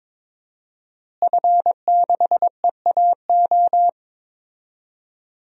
Morse code, heard as a keyed tone, about 22 wpm, 700 Hz. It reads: F6EAO
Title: DXCC-100 Call Signs - 5 Characters - Repeated 2x - 22wpm